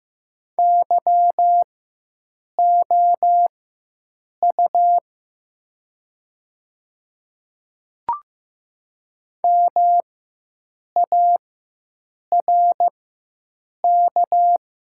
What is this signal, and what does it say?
Morse code, heard as a keyed tone, about 15 wpm, spaced as for 7 wpm, 700 Hz. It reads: YOU MARK
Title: 4x Character Spacing - Top 500 Words - 15wpm